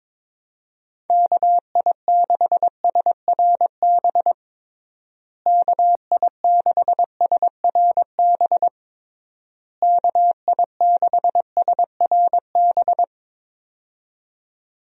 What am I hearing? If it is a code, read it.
Morse code, heard as a keyed tone, about 21 wpm, 700 Hz. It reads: KI6SRB KI6SRB KI6SRB
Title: USA Call Signs - 6 Characters - Repeated 3x - 22wpm